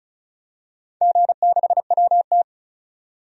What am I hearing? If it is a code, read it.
Morse code, heard as a keyed tone, about 34 wpm, 700 Hz. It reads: G6WT
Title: DXCC-100 Call Signs - 4 Characters - 35wpm